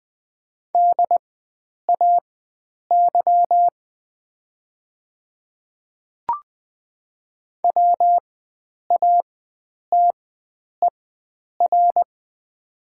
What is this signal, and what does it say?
Morse code, heard as a keyed tone, about 20 wpm, spaced as for 9 wpm, 700 Hz. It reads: DAY WATER
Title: 4x Character Spacing - Top 100 Words - 20wpm